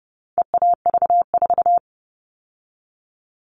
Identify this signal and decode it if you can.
Morse code, heard as a keyed tone, about 30 wpm, 700 Hz. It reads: EAV4
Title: All Cut Numbers - 4-Digit 30wpm